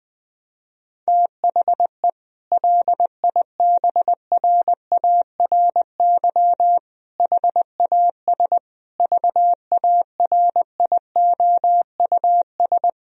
Morse code, heard as a keyed tone, about 20 wpm, 700 Hz. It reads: THE LIBRARY HAS VARIOUS
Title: Sentences from Top 1000 Words - Review Part 4 - 20wpm